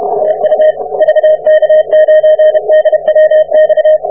Morse code, heard as a keyed tone, about 32 wpm, 620 Hz. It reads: TU VK9DWX